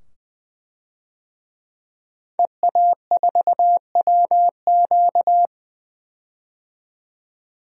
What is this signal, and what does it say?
Morse code, heard as a keyed tone, about 20 wpm, 700 Hz. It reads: EA4WQ
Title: DXCC-100 Call Signs - 5 Characters - 20wpm